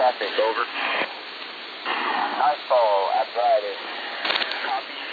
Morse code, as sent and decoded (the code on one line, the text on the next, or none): none